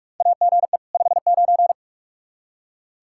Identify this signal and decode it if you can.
Morse code, heard as a keyed tone, about 46 wpm, 700 Hz. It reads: AGE 59
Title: Practice QSO - Exchange 3 - More Info - 45wpm